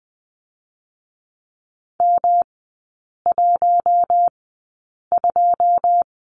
Morse code, heard as a keyed tone, about 20 wpm, spaced as for 8 wpm, 700 Hz. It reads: M12
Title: 152 Lesson 25e - Letter p - Three Characters 20wpm